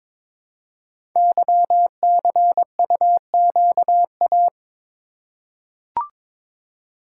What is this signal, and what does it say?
Morse code, heard as a keyed tone, about 22 wpm, 700 Hz. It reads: YCUQA
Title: Code Groups - 5 Letters - 22wpm